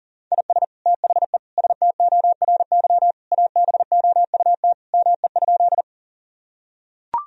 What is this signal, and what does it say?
Morse code, heard as a keyed tone, about 40 wpm, 700 Hz. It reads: IS THE STORY ABOUT ME?
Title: Sentences from Top 1000 Words - Review Part 1 - 40wpm